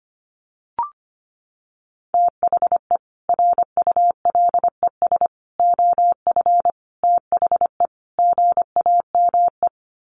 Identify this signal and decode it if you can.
Morse code, heard as a keyed tone, about 25 wpm, 700 Hz. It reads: THE RULES OF THE GAME
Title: Sets of 5 Words Encore2 25wpm